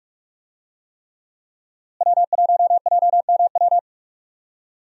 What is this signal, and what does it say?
Morse code, heard as a keyed tone, about 46 wpm, 700 Hz. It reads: W1JMW
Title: USA Call Signs - 5 Characters - Repeated 3x - 45wpm